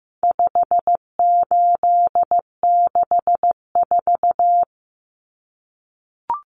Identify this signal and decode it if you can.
Morse code, heard as a keyed tone, about 15 wpm, 700 Hz. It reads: 5864
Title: Numbers - 4-Digit 15wpm